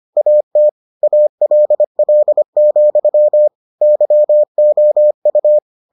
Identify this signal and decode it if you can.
Morse code, heard as a keyed tone, about 25 wpm, 600 Hz. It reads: AT ALL, YOU